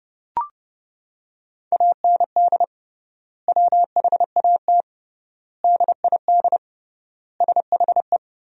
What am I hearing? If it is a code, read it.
Morse code, heard as a keyed tone, about 30 wpm, 700 Hz. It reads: AND WHAT DID SHE
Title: 3x Word-Spacing - Sentences from Top 200 Words - 30wpm